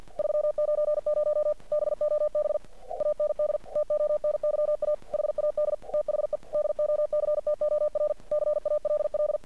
Morse code, heard as a keyed tone, about 50 wpm, 600 Hz. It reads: 390 BOB AND TONYA HAD THE DOCTOR CALL
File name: archived / CRSnet